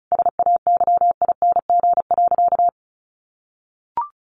Morse code, heard as a keyed tone, about 34 wpm, 700 Hz. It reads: SAYING.
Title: A Sound Of Thunder 35wpm